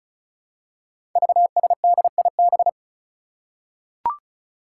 Morse code, heard as a keyed tone, about 35 wpm, 700 Hz. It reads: VSDIB
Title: Five Challenging Letter-Number Combinations 35wpm